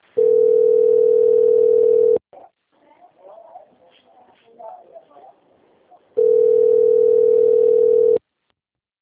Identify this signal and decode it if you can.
Ringback tone (North America), 440 + 480 Hz.